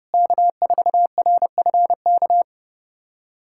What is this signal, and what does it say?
Morse code, heard as a keyed tone, about 30 wpm, 700 Hz. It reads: K4RFK
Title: USA Callsign DE Callsign - 5 Characters - 30wpm